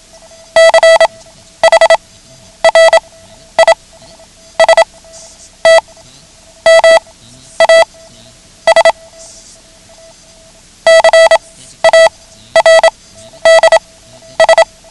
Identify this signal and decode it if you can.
Morse code, heard as a keyed tone, about 27 wpm, spaced as for 11 wpm, 660 Hz. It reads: CHRISTMAS CARDS
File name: ham radio / CodeQuick / Code Quick 09